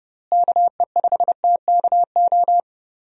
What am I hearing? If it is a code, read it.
Morse code, heard as a keyed tone, about 30 wpm, 700 Hz. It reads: KE5TKO
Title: USA Call Signs - 6 Characters - Repeated 2x - 30wpm